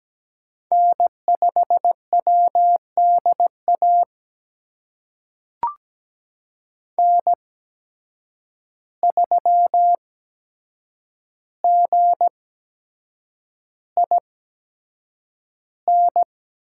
Morse code, heard as a keyed tone, about 17 wpm, spaced as for 7 wpm, 700 Hz. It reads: N5WDA N3GIN
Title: USA Call Signs - 5 Characters - 8x Character Spacing - 17wpm